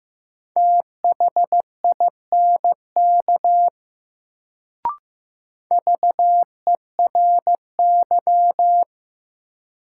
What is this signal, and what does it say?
Morse code, heard as a keyed tone, about 15 wpm, 700 Hz. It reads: THINK VERY
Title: Sentences from Top 200 Words - New Word Drill - 15wpm